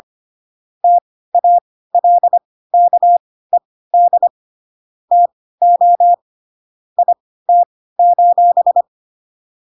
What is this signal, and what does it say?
Morse code, heard as a keyed tone, about 25 wpm, spaced as for 16 wpm, 700 Hz. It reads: TALKED TO IT:
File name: Pinocchio / Pinocchio ch5 10 25 ik1hge